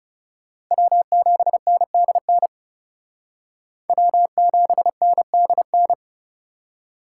Morse code, heard as a keyed tone, about 34 wpm, 700 Hz. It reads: W7NDN W7NDN
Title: Speed Racing - USA Call Signs 30wpm